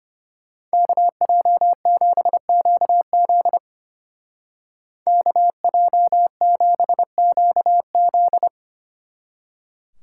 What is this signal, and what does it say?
Morse code, heard as a keyed tone, about 30 wpm, 700 Hz. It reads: KJ7QZ KJ7QZ
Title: Speed Racing - USA 2x2 Call Signs 25wpm